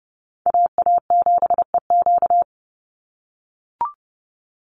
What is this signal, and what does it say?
Morse code, heard as a keyed tone, about 30 wpm, 700 Hz. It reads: AA7EQ
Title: NA SOTA Chaser Callsigns 30wpm